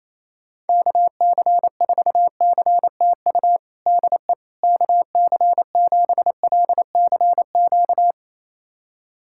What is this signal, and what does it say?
Morse code, heard as a keyed tone, about 28 wpm, 700 Hz. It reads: KC4CTU DE KC7LCQ
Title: USA Callsign DE Callsign - 6 Characters - 28wpm